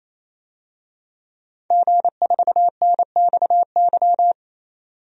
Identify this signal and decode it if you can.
Morse code, heard as a keyed tone, about 28 wpm, 700 Hz. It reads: G4NXY
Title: DXCC-100 Call Signs - 5 Characters - 28wpm